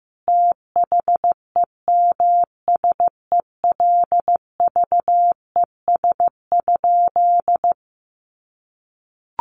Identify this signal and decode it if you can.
Morse code, heard as a keyed tone, about 15 wpm, 700 Hz. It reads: THEMSELVES?
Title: Sentences from Top 800 Words 15wpm